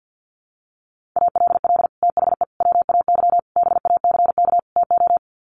Morse code, heard as a keyed tone, about 50 wpm, 700 Hz. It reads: ALL THE WAY BACK TO
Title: Sets of 5 Words Encore 50wpm